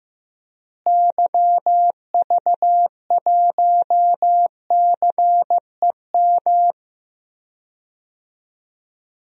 Morse code, heard as a keyed tone, about 15 wpm, 700 Hz. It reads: YV1CEM
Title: DXCC-100 Call Signs - 6 Characters - Repeated 3x - 15wpm